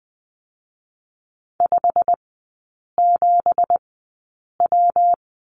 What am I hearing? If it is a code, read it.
Morse code, heard as a keyed tone, about 20 wpm, spaced as for 8 wpm, 700 Hz. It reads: 57W
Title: 166 Lesson 27e - Number 7 - Three Characters 20wpm